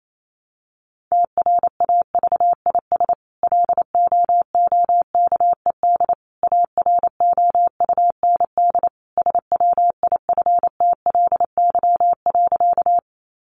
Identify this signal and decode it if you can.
Morse code, heard as a keyed tone, about 28 wpm, 700 Hz. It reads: TRAVIS LOOKED AROUND SWIFTLY.
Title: A Sound Of Thunder 28wpm